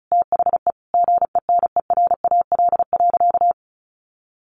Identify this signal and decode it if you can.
Morse code, heard as a keyed tone, about 34 wpm, 700 Hz. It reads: THE GENERAL.
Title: Sentences from Top 600 Words 35wpm